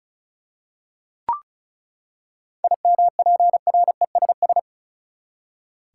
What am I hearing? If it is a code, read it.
Morse code, heard as a keyed tone, about 36 wpm, 700 Hz. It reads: IMPRESS
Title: Seven Letter Words - 35wpm